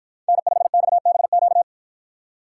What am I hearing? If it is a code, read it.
Morse code, heard as a keyed tone, about 52 wpm, 700 Hz. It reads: N5XBQ